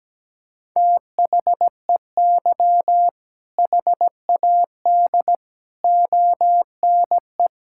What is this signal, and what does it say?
Morse code, heard as a keyed tone, about 17 wpm, 700 Hz. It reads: THEY HAD ONE
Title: Sentences from Top 1600 Words 17wpm